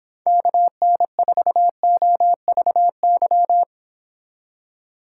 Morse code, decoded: KN4OVY

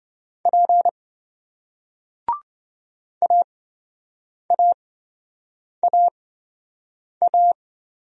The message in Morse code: P AAAA

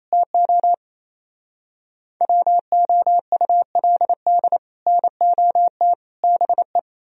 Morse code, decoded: TO WOULD NOT BE

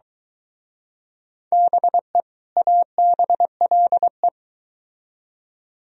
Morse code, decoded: BE ABLE

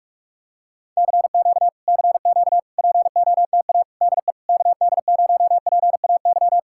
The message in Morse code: CQ CQ POTA DE KD0PAY

